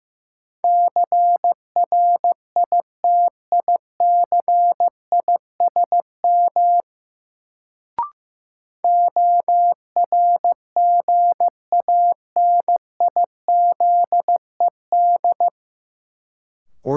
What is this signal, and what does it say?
Morse code, heard as a keyed tone, about 15 wpm, 700 Hz. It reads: CRITICISM ORGANIZED
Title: Sentences from Top 1300 Words - New Word Drill Alt - 15wpm